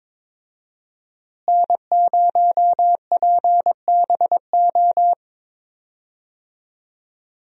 Morse code, heard as a keyed tone, about 21 wpm, 700 Hz. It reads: N0PBO
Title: USA Call Signs - 5 Characters - Repeated 2x - 22wpm